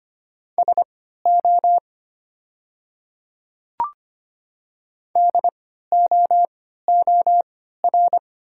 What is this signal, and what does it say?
Morse code, heard as a keyed tone, about 25 wpm, spaced as for 14 wpm, 700 Hz. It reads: SO DOOR
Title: 3x Character Spacing - Top 300 Words - 25wpm